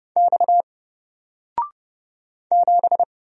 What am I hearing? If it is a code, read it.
Morse code, heard as a keyed tone, about 30 wpm, 700 Hz. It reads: X 7
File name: Single-Letter-Number-30wpm